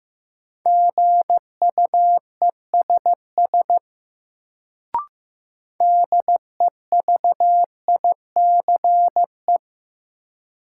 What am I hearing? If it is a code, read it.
Morse code, heard as a keyed tone, about 15 wpm, 700 Hz. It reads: GUESS DEVICE
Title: Sentences from Top 1500 Words - New Word Drill - 15wpm